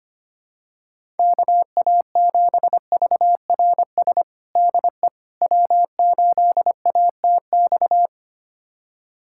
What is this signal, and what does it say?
Morse code, heard as a keyed tone, about 25 wpm, 700 Hz. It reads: KA7VRS DE W8ATX
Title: USA Callsign DE Callsign 25wpm